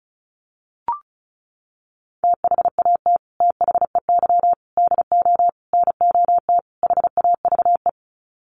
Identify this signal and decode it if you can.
Morse code, heard as a keyed tone, about 34 wpm, 700 Hz. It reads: THAT THEY DO NOT HAVE
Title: Sets of 5 Words Encore2 35wpm